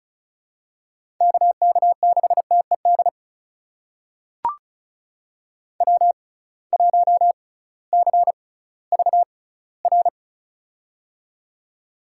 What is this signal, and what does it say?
Morse code, heard as a keyed tone, about 36 wpm, spaced as for 19 wpm, 700 Hz. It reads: KK6TED W1CVR
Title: USA Call Signs - 6x Character Spacing - 35wpm